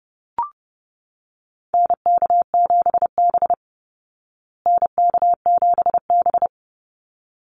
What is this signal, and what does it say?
Morse code, heard as a keyed tone, about 30 wpm, 700 Hz. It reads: NK7B NK7B
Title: Call Signs Repeated 3x - 30wpm